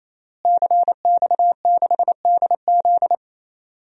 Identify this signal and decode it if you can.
Morse code, heard as a keyed tone, about 28 wpm, 700 Hz. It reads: CX6DZ
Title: DXCC-100 Call Signs - 5 Characters - Repeated 3x - 28wpm